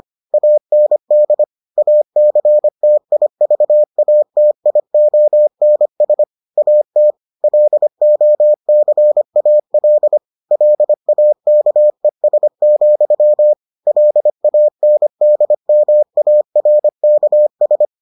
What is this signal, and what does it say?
Morse code, heard as a keyed tone, about 25 wpm, 600 Hz. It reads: AND ACTIVATIONS AT LOCAL LAKES, LANDMARKS